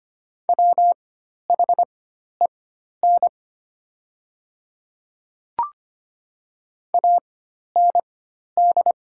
Morse code, heard as a keyed tone, about 25 wpm, spaced as for 12 wpm, 700 Hz. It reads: WHEN AND
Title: 4x Character Spacing - Top 100 Words - 25wpm